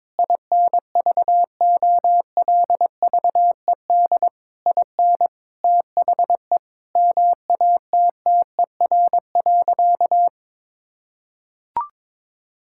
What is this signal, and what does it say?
Morse code, heard as a keyed tone, about 22 wpm, 700 Hz. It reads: INVOLVED IN THE MATTER.
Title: Sentences from Top 1000 Words - Review Part 5 - 22wpm